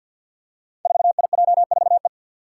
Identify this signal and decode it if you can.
Morse code, heard as a keyed tone, about 52 wpm, 700 Hz. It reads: 4IJ4E
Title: Five Challenging Letter-Number Combinations 50wpm